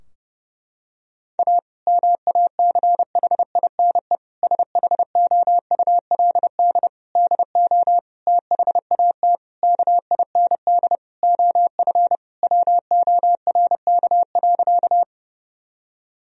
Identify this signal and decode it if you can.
Morse code, heard as a keyed tone, about 30 wpm, 700 Hz. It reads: A MACHINE SHOULD DO THAT KIND OF WORK.
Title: Sentences from Top 1000 Words - Review Part 2 - 30wpm